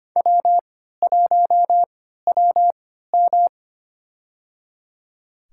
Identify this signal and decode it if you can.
Morse code, heard as a keyed tone, about 25 wpm, spaced as for 14 wpm, 700 Hz. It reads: W1WM
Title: USA Call Signs - 4 Characters - 3x Character Spacing - 25wpm